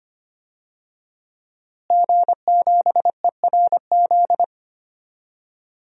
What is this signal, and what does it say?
Morse code, heard as a keyed tone, about 25 wpm, 700 Hz. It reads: G7ERZ
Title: DXCC-100 Call Signs - 5 Characters - Repeated 3x - 25wpm